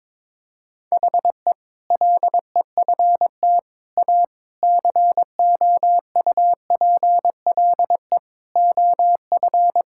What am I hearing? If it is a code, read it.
Morse code, heard as a keyed tone, about 22 wpm, 700 Hz. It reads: HE LEFT A COUPLE OF
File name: practice / 22wpm / Sentences with 4-Word Phrases - 22wpm